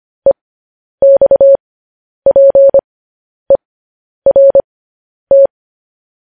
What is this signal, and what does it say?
Morse code, heard as a keyed tone, about 25 wpm, spaced as for 10 wpm, 550 Hz. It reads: EXPERT
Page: news-headlines-in-morse-code-at-10-wpm